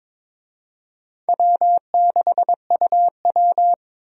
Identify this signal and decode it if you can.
Morse code, heard as a keyed tone, about 22 wpm, 700 Hz. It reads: W6UW
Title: USA Call Signs - 4 Characters - Repeated 3x - 22wpm